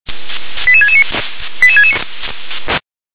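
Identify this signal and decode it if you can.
Five-tone selective call (ZVEI), heard as quick runs of five tones, tones of about 70 ms. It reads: 88669 88669
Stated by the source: ZVEI five-tone selective call received by SDR.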